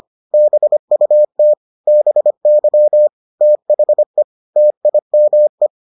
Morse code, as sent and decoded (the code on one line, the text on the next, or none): -... ..- - / -... -.-- / - .... / - .. -- .
BUT BY THE TIME